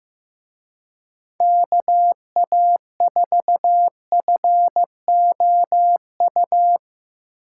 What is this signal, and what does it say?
Morse code, heard as a keyed tone, about 16 wpm, 700 Hz. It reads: KA4FOU
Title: Speed Racing - USA 2x3 Call Signs 15wpm